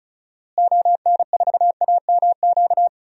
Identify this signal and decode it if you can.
Morse code, heard as a keyed tone, about 34 wpm, 700 Hz. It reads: ON4AMQ
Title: DXCC-100 Call Signs - 6 Characters - Repeated 2x - 35wpm